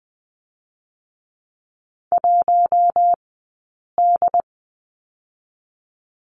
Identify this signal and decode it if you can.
Morse code, heard as a keyed tone, about 20 wpm, 700 Hz. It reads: 1 D
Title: 132 Lesson 22d - Question Mark - Two Characters 20wpm